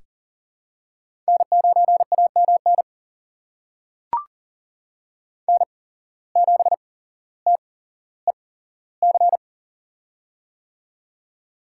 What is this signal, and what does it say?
Morse code, heard as a keyed tone, about 40 wpm, spaced as for 17 wpm, 700 Hz. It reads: N9AMN N7TEC
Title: USA Call Signs - 5 Characters - 8x Character Spacing - 40wpm